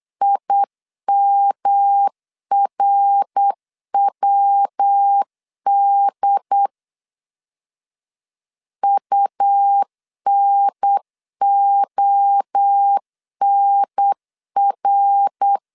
Morse code, heard as a keyed tone, about 8 wpm, 800 Hz. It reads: IMRWD UNONR